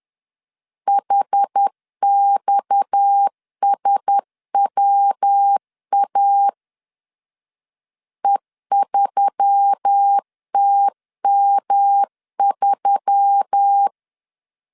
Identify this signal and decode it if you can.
Morse code, heard as a keyed tone, about 10 wpm, 800 Hz. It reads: HXSWA E3TM3